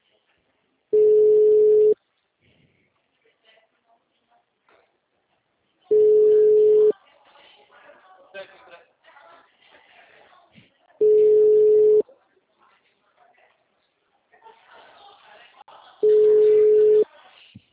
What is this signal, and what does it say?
Ringback tone (Europe, 425 Hz), with speech.
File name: out- -2009- - - .490